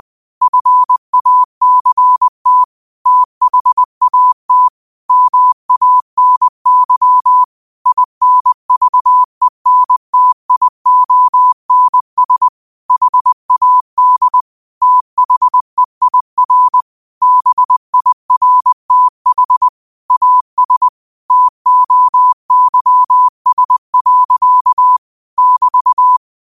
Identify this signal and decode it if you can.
Morse code, heard as a keyed tone, about 20 wpm, 1000 Hz. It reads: FACT THAT MANY INVENTIONS HAD THEIR BIRTH AS TOYS. <BT>